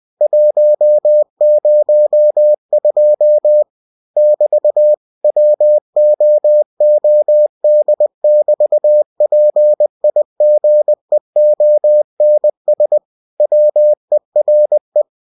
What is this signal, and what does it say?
Morse code, heard as a keyed tone, about 20 wpm, 600 Hz. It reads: 102 <BT> WOOD<BT>PIGEONS WERE
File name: cw-books / winnie-the-pooh / winnie-20wpm-page-0102